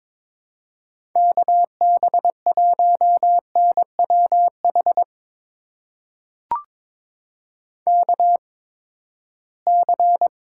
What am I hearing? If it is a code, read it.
Morse code, heard as a keyed tone, about 22 wpm, spaced as for 15 wpm, 700 Hz. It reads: KB1NWH KC
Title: USA Call Signs - 6 Characters - 8x Character Spacing - 22wpm